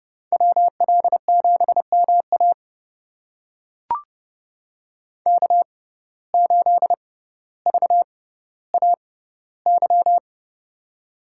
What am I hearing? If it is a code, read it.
Morse code, heard as a keyed tone, about 29 wpm, spaced as for 15 wpm, 700 Hz. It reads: WL7MA K8VAY
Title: USA Call Signs - 5 Characters - 6x Character Spacing - 30wpm